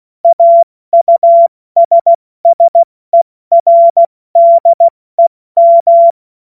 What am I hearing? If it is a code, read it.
Morse code, heard as a keyed tone, about 16 wpm, 680 Hz. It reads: AUSSERDEM
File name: Morsekurs / Witze / Witz 29. Juni 2021, 70 F80